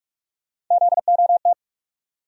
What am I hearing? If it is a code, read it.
Morse code, heard as a keyed tone, about 48 wpm, 700 Hz. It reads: ZOT
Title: USA Call Sign Postfixes - 3 Letters 45wpm